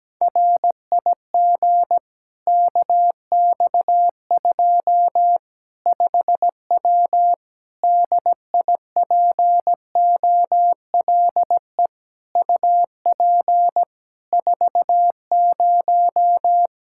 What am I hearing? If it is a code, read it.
Morse code, heard as a keyed tone, about 17 wpm, 700 Hz. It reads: RIG KX2 5W DIPOLE UP 40
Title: Practice QSO - Exchange 2 - Info - 17wpm